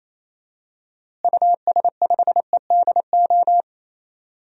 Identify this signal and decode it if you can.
Morse code, heard as a keyed tone, about 28 wpm, 700 Hz. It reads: US5EDO
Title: DXCC-100 Call Signs - 6 Characters - Repeated 2x - 28wpm